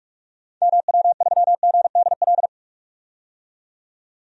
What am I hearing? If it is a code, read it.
Morse code, heard as a keyed tone, about 42 wpm, 700 Hz. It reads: MW3GDL
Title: DXCC-100 Call Signs - 6 Characters - 45wpm